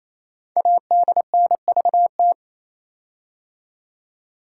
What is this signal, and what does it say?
Morse code, heard as a keyed tone, about 27 wpm, 700 Hz. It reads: ADNVT
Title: Code Groups - 5 Letters - 28wpm